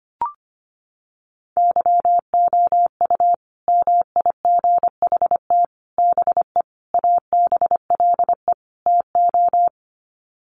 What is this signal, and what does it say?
Morse code, heard as a keyed tone, about 25 wpm, 700 Hz. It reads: YOU MIGHT BE ABLE TO
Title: Sets of 5 Words Encore2 25wpm